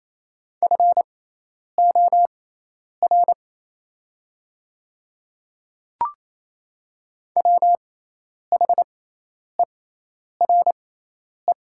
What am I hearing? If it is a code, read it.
Morse code, heard as a keyed tone, about 28 wpm, spaced as for 10 wpm, 700 Hz. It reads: FOR WHERE
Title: 6x Character Spacing - Top 200 Words - 28wpm